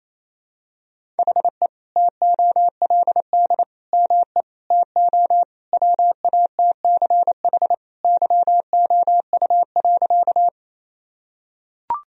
Morse code, heard as a keyed tone, about 28 wpm, 700 Hz. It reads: HE TOLD ME TO WATCH YOU.
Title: Sentences from Top 1000 Words - Review Part 1 - 28wpm